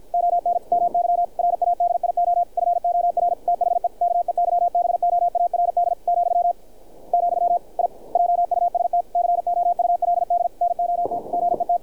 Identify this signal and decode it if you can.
Morse code, heard as a keyed tone, about 50 wpm, 700 Hz, with radio noise. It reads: ON MY RADIO FOR THE KEYBOARD , , I JUST COULD NOT GET
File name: archived / CRSnet